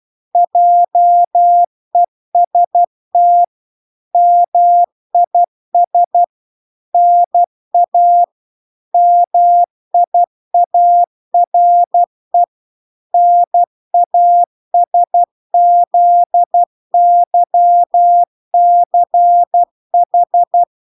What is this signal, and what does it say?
Morse code, heard as a keyed tone, about 12 wpm, 700 Hz. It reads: JEST MIS NA MIARE NASZYCH